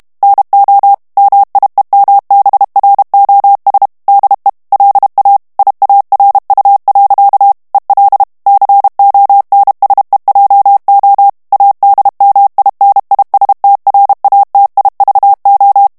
Morse code, heard as a keyed tone, about 32 wpm, 800 Hz. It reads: NO MIEMBROS DE LA IARU. EL CONSEJO ADMINISTRATIVO
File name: morse mp3 / ESPAGNOL TEXTE EN CLAIR 1 VITESSE 30 MOTS MINUTES